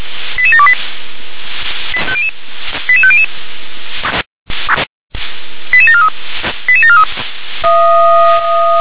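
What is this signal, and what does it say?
Five-tone selective call (ZVEI), heard as quick runs of five tones, tones of about 70 ms. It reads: 88728 88500 88743 88743
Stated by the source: ZVEI five-tone call 88631 received on an SDR.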